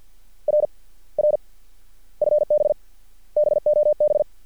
Morse code, heard as a keyed tone, about 50 wpm, spaced as for 31 wpm, 600 Hz. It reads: RR FB BOB
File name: archived / CRSnet